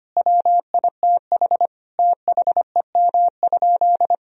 Morse code, heard as a keyed tone, about 25 wpm, 700 Hz. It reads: WITH THEM?
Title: Sentences from Top 1000 Words - Review Part 3 - 25wpm